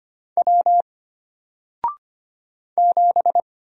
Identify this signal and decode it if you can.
Morse code, heard as a keyed tone, about 25 wpm, 700 Hz. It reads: W 7
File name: Single-Letter-Number-25wpm